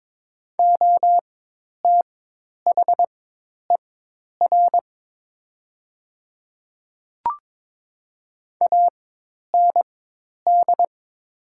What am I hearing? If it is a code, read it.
Morse code, heard as a keyed tone, about 22 wpm, spaced as for 10 wpm, 700 Hz. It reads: OTHER AND
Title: 4x Character Spacing - Top 100 Words - 22wpm